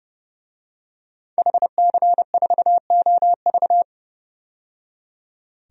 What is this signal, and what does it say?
Morse code, heard as a keyed tone, about 30 wpm, 700 Hz. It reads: HC4OV
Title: DXCC-100 Call Signs - 5 Characters - 30wpm